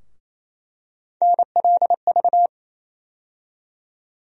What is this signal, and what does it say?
Morse code, heard as a keyed tone, about 27 wpm, 700 Hz. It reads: NLV
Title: USA Call Sign Postfixes - 3 Letters 28wpm